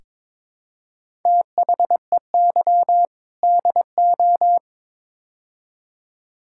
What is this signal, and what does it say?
Morse code, heard as a keyed tone, about 22 wpm, 700 Hz. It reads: THEY DO
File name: practice / 22wpm / Build A Phrase - 5-Word Phrases - 22wpm